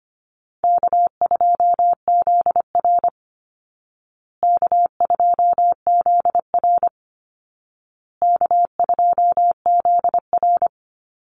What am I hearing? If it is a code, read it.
Morse code, heard as a keyed tone, about 25 wpm, 700 Hz. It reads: K2ZR K2ZR K2ZR
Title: Call Signs Repeated 2x - 25wpm